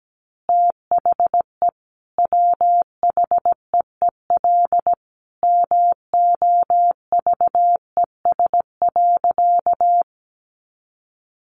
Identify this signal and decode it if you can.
Morse code, heard as a keyed tone, about 17 wpm, 700 Hz. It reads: THE WHEEL MOVES.
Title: Additional Easy Phrases 17wpm